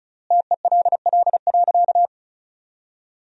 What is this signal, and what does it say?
Morse code, heard as a keyed tone, about 36 wpm, 700 Hz. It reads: TELL.
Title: 3x Word-Spacing - Sentences from Top 300 Words - 35wpm